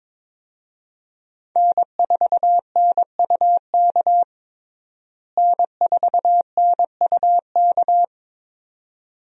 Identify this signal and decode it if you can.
Morse code, heard as a keyed tone, about 22 wpm, 700 Hz. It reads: N4NUK N4NUK
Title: USA Call Signs - 5 Characters - Repeated 2x - 22wpm